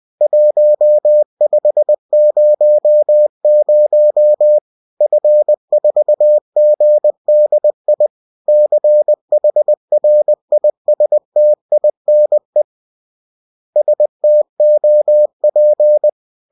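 Morse code, heard as a keyed tone, about 20 wpm, 600 Hz. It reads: 1500 F4GDI CHRISTINE STOP
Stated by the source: Bulletin sent in CW.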